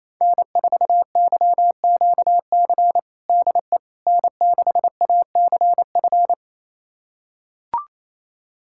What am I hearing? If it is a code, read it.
Morse code, heard as a keyed tone, about 28 wpm, 700 Hz. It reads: N4YQC DE N6ACF
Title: USA Callsign DE Callsign - 5 Characters - 28wpm